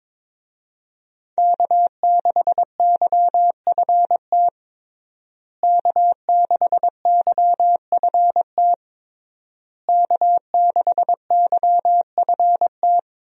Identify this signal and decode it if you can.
Morse code, heard as a keyed tone, about 21 wpm, 700 Hz. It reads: K6YFT K6YFT K6YFT
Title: USA Call Signs - 5 Characters - Repeated 3x - 22wpm